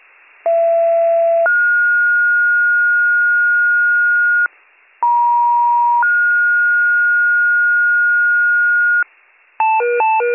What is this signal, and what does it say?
Two-tone page, heard as a one-second tone, then a three-second tone, about 670 then 1430 Hz, about 950 then 1430 Hz. It is heard from 0.5 to 9.0 s.